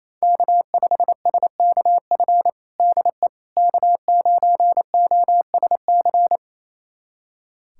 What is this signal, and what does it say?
Morse code, heard as a keyed tone, about 28 wpm, 700 Hz. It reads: K5SKF DE K9OSC
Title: USA Callsign DE Callsign - 5 Characters - 28wpm